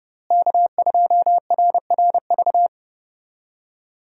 Morse code, heard as a keyed tone, about 32 wpm, 700 Hz. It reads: K2RRV
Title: USA Call Signs - 5 Characters - Repeated 3x - 30wpm